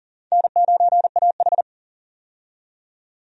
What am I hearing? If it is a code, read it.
Morse code, heard as a keyed tone, about 40 wpm, 700 Hz. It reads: N9AH
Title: USA Call Signs - 4 Characters - 40wpm